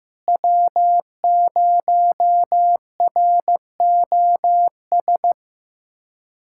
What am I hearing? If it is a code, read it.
Morse code, heard as a keyed tone, about 15 wpm, 700 Hz. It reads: W0ROS
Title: USA Callsign DE Callsign - 5 Characters - 15wpm